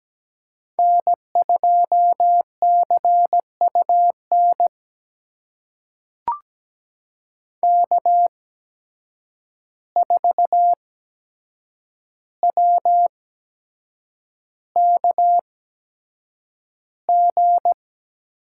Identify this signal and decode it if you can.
Morse code, heard as a keyed tone, about 17 wpm, spaced as for 7 wpm, 700 Hz. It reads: N2CUN K4WKG
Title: USA Call Signs - 5 Characters - 8x Character Spacing - 17wpm